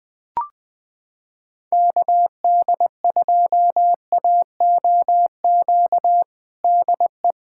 Morse code, heard as a keyed tone, about 20 wpm, 700 Hz. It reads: KD2AOQ DE
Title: USA Callsign DE Callsign - 6 Characters - 20wpm